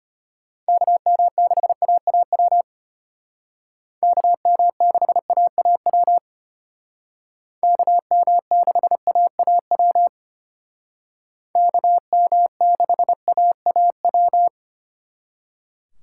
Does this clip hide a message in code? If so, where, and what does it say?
Yes: Morse code from 0.7 to 14.5 s, heard as a keyed tone, about 32 wpm, 700 Hz. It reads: KM6AAW KM6AAW KM6AAW KM6AAW